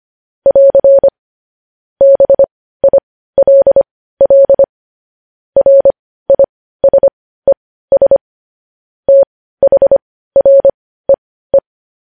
Morse code, heard as a keyed tone, about 25 wpm, spaced as for 15 wpm, 550 Hz. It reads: <AR> BILL RISES THREE
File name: podcast / News headlines-1x-25x15